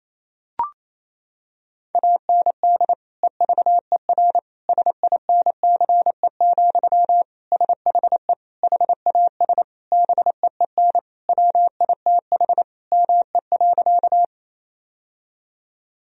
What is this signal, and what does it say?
Morse code, heard as a keyed tone, about 28 wpm, 700 Hz. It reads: AND EVER SINCE, SHE HAS BEEN WITH ME.
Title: Sentences from Top 1000 Words - Review Part 1 - 28wpm